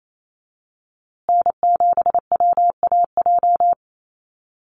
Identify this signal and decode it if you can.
Morse code, heard as a keyed tone, about 28 wpm, 700 Hz. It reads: N7WAJ
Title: USA Call Signs Repeated 2x 28wpm